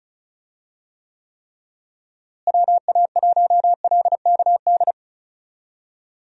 Morse code, heard as a keyed tone, about 36 wpm, 700 Hz. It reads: WA1LKD
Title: USA Call Signs - 6 Characters - Repeated 3x - 35wpm